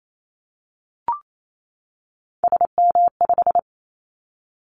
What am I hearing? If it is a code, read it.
Morse code, heard as a keyed tone, about 28 wpm, 700 Hz. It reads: SM5
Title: Callsign Prefixes for Top 25 DXCC - 28wpm